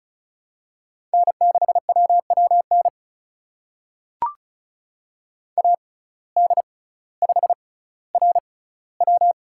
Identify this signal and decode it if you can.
Morse code, heard as a keyed tone, about 34 wpm, spaced as for 18 wpm, 700 Hz. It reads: N6WWN AD5RW
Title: USA Call Signs - 5 Characters - 6x Character Spacing - 35wpm